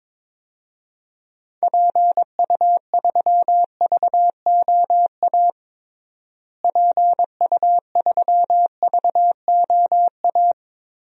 Morse code, heard as a keyed tone, about 22 wpm, 700 Hz. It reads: PU3VOA PU3VOA
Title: DXCC-100 Call Signs - 6 Characters - Repeated 3x - 22wpm